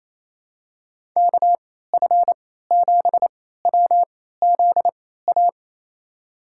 Morse code, heard as a keyed tone, about 27 wpm, spaced as for 16 wpm, 700 Hz. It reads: KF7WZA
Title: USA Call Signs - 6 Characters - 3x Character Spacing - 28wpm